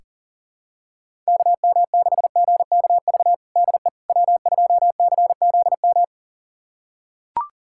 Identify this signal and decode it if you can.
Morse code, heard as a keyed tone, about 40 wpm, 700 Hz. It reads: KM6GKV DE W2CZM
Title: USA Callsign DE Callsign 40wpm